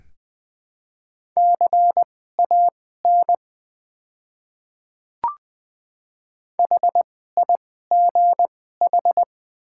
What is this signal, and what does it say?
Morse code, heard as a keyed tone, about 21 wpm, spaced as for 15 wpm, 700 Hz. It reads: CAN HIGH
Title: 2x Character Spacing - Top 200 Words - 20wpm